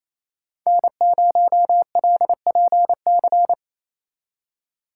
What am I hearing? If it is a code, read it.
Morse code, heard as a keyed tone, about 34 wpm, 700 Hz. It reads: N0LPC
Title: Speed Racing - USA 1x3 Call Signs 28wpm